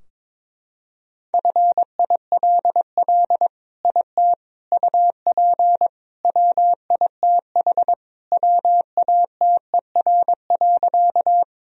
Morse code, heard as a keyed tone, about 22 wpm, 700 Hz. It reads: FILL IT UP WITH WATER.
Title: Sentences from Top 1000 Words - Review Part 3 - 22wpm